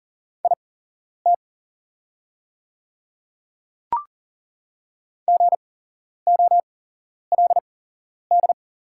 Morse code, heard as a keyed tone, about 40 wpm, spaced as for 11 wpm, 700 Hz. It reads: IT GOLD
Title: 8x Character Spacing - Top 500 Words - 40wpm